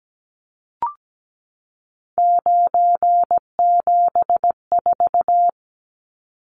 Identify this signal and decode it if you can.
Morse code, heard as a keyed tone, about 17 wpm, 700 Hz. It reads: 974
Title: Numbers - 3-Digit 17wpm